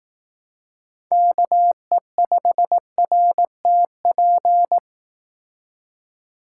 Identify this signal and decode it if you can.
Morse code, heard as a keyed tone, about 18 wpm, 700 Hz. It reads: KE5RTP
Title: Speed Racing - USA 2x3 Call Signs 15wpm